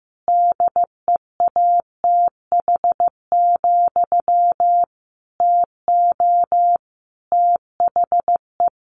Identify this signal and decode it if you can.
Morse code, heard as a keyed tone, about 15 wpm, 700 Hz. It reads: DEATH, TO THE
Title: A Sound Of Thunder 15wpm